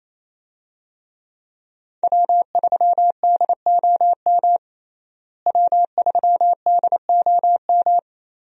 Morse code, heard as a keyed tone, about 28 wpm, 700 Hz. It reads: W3DOM W3DOM
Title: USA Call Signs - 5 Characters - Repeated 2x - 28wpm